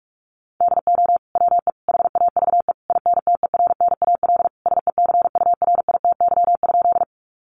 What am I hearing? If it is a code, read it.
Morse code, heard as a keyed tone, about 44 wpm, 700 Hz. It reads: DO WE HAVE INTERNAL SECURITY?
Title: Sentences from Top 1400 Words 45wpm